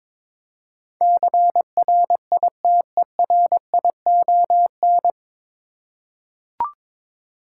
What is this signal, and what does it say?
Morse code, heard as a keyed tone, about 22 wpm, 700 Hz. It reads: CRITERION
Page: Morse Code Ninja practice 22wpm